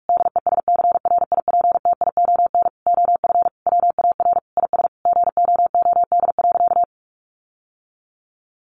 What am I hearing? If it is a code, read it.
Morse code, heard as a keyed tone, about 44 wpm, 700 Hz. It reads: DESCRIPTION OF WAR IS GOOD.
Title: Sentences from Top 1300 Words 45wpm